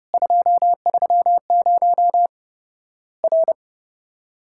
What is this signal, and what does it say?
Morse code, heard as a keyed tone, about 32 wpm, 700 Hz. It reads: 230 R
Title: Worked All Europe DX Contest - QTC Exchange - 30wpm